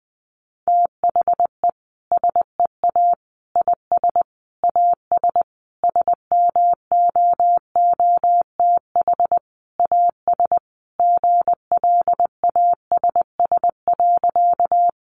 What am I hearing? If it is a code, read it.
Morse code, heard as a keyed tone, about 20 wpm, 700 Hz. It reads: THE SEA IS AS SMOOTH AS GLASS.
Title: Sentences from Top 1400 Words 20wpm